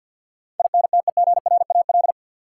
Morse code, heard as a keyed tone, about 53 wpm, 700 Hz. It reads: INTEGRAL
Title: Sets of 5 Words - New Word Drill - 50wpm